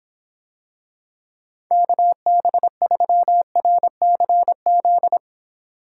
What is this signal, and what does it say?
Morse code, heard as a keyed tone, about 26 wpm, 700 Hz. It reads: KB3RCZ